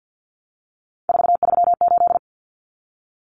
Morse code, heard as a keyed tone, about 50 wpm, 700 Hz. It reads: 438
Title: Common Cut Numbers - 3-Digit 50wpm